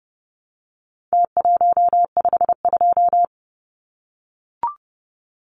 Morse code, heard as a keyed tone, about 30 wpm, 700 Hz. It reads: T152
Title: Numbers - Contest Style - 4-Digit 30wpm